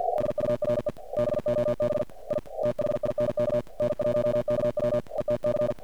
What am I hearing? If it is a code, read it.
Morse code, heard as a keyed tone, about 53 wpm, 600 Hz, with noise. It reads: DURE BOB I THINK N1KW ETC
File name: archived / CRSnet